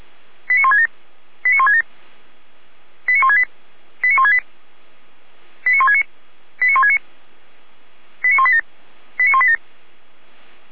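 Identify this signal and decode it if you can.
Five-tone selective call (ZVEI), heard as quick runs of five tones, tones of about 80 ms. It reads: 89167 89167 89168 89168 89169 89169 89187 89187
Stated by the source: ZVEI five-tone selective call received by SDR.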